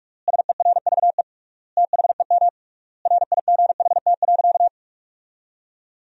Morse code, heard as a keyed tone, about 46 wpm, 700 Hz. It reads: SERVE THEM RIGHT.
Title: 3x Word-Spacing - Sentences from Top 400 Words - 45wpm